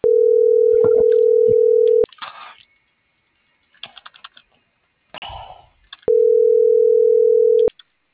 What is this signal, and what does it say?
Ringback tone (North America), 440 + 480 Hz.